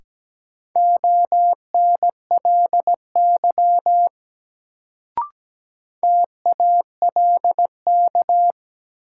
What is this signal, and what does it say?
Morse code, heard as a keyed tone, about 17 wpm, 700 Hz. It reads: ONLY TALK
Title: Sets of 2 Words - New Word Drill - 17wpm